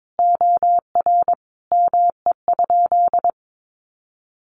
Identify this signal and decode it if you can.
Morse code, heard as a keyed tone, about 22 wpm, 700 Hz. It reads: OR ME?
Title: Sentences from Top 200 Words 22wpm